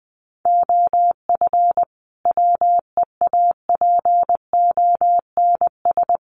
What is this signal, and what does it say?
Morse code, heard as a keyed tone, about 20 wpm, 700 Hz. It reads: OF WEAPONS
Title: Sets of 5 Words Encore 20wpm